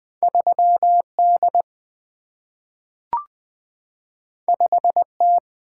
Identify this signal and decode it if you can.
Morse code, heard as a keyed tone, about 20 wpm, 700 Hz. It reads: 3D 5T
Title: Single-Letter Single-Number 20wpm